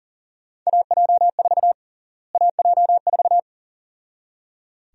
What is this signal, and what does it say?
Morse code, heard as a keyed tone, about 39 wpm, 700 Hz. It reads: AJ4 AJ4
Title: USA Call Sign Prefixes - Repeated 2x - 40wpm